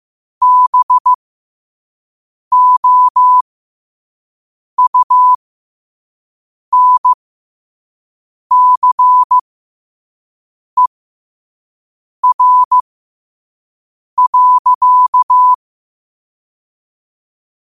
Morse code, heard as a keyed tone, about 15 wpm, spaced as for 5 wpm, 1000 Hz. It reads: BOUNCER.
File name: qotdfiles / 5wpm